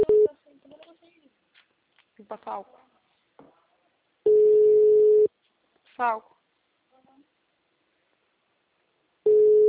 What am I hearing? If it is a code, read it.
Ringback tone (Europe, 425 Hz), with speech.